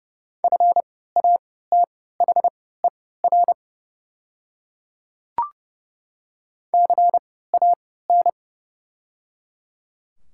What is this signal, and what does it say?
Morse code, heard as a keyed tone, about 30 wpm, spaced as for 17 wpm, 700 Hz. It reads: FATHER CAN
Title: 3x Character Spacing - Top 300 Words - 30wpm